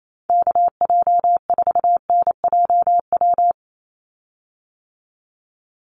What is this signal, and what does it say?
Morse code, heard as a keyed tone, about 28 wpm, 700 Hz. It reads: KJ4NJW
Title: USA Call Signs 28wpm